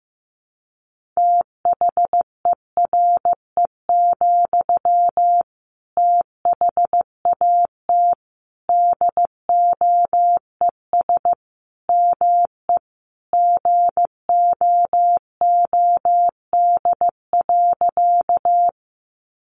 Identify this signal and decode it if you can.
Morse code, heard as a keyed tone, about 15 wpm, 700 Hz. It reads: THERE, THAT DOES ME GOOD.
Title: Sentences from Top 200 Words 15wpm